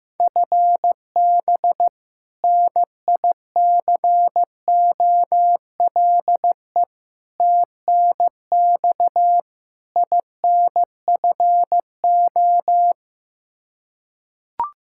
Morse code, heard as a keyed tone, about 15 wpm, 700 Hz. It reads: FB NICOLE TNX INFO
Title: Practice QSO - Exchange 3 - More Info - 15wpm